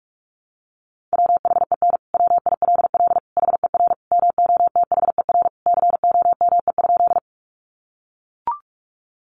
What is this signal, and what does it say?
Morse code, heard as a keyed tone, about 44 wpm, 700 Hz. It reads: WHEN WILL HER MOTHER COME?
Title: Sentences from Top 200 Words 45wpm